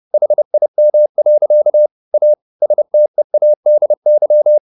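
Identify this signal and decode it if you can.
Morse code, heard as a keyed tone, about 30 wpm, 600 Hz. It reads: HIM. A STEADY
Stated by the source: Morse code at 30 wpm, the novel Never by Ken Follett in English, from CW med Gnister.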